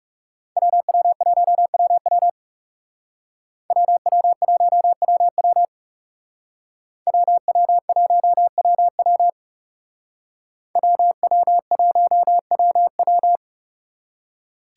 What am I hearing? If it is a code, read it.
Morse code, heard as a keyed tone, about 35 wpm, 700 Hz. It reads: WW1WW WW1WW WW1WW WW1WW
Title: Speed Racing - USA 2x2 Call Signs 30wpm